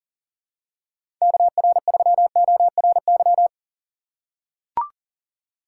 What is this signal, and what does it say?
Morse code, heard as a keyed tone, about 39 wpm, 700 Hz. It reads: KR3ORY
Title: USA Call Signs - 6x Character Spacing - 40wpm